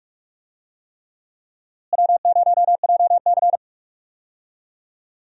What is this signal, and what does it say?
Morse code, heard as a keyed tone, about 46 wpm, 700 Hz. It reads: W0JC
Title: USA Call Signs - 4 Characters - Repeated 3x - 45wpm